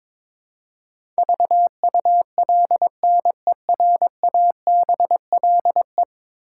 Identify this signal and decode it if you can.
Morse code, heard as a keyed tone, about 22 wpm, 700 Hz. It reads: VULNERABLE
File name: practice / 22wpm / 10 Letter Words - First 3 Letters Given - 22wpm